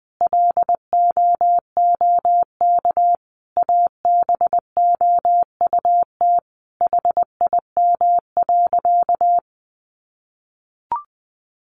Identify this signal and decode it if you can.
Morse code, heard as a keyed tone, about 20 wpm, 700 Hz. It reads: LOOK ABOUT HIM.
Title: Sentences from Top 200 Words 20wpm